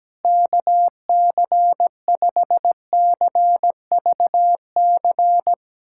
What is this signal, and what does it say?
Morse code, heard as a keyed tone, about 17 wpm, 700 Hz. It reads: KC5CVC
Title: USA Callsign DE Callsign 17wpm